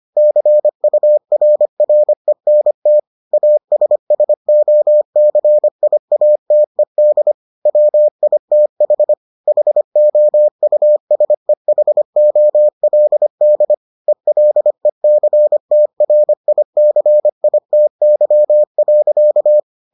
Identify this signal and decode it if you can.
Morse code, heard as a keyed tone, about 25 wpm, 600 Hz. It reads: CURRENT ASSOCIATED WITH HOUSEHOLD ELECTRICITY.